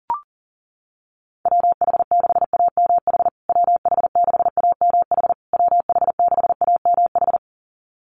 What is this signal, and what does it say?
Morse code, heard as a keyed tone, about 40 wpm, 700 Hz. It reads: WH6AMH WH6AMH WH6AMH
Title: USA Call Signs Repeated 3x 40wpm